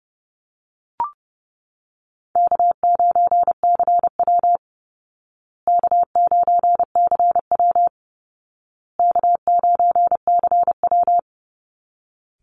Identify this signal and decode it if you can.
Morse code, heard as a keyed tone, about 30 wpm, 700 Hz. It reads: K9CW K9CW K9CW
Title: Easy Call Signs Repeated 3x - 30wpm